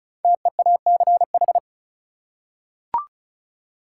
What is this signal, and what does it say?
Morse code, heard as a keyed tone, about 36 wpm, 700 Hz. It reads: TEACH
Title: Five Letter Words - 35wpm